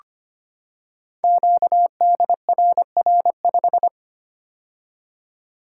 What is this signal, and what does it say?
Morse code, heard as a keyed tone, about 25 wpm, 700 Hz. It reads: QDRR5
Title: Code Groups - 5 Letter:Numbers - 25wpm